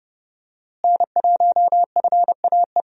Morse code, heard as a keyed tone, about 31 wpm, 700 Hz. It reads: N1FAE
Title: USA Call Signs - 5 Characters - Repeated 3x - 30wpm